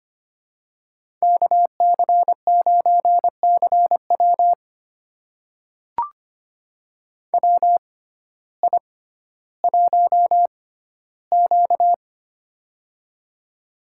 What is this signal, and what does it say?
Morse code, heard as a keyed tone, about 25 wpm, spaced as for 14 wpm, 700 Hz. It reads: KC9CW WI1Q
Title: USA Call Signs - 6x Character Spacing - 25wpm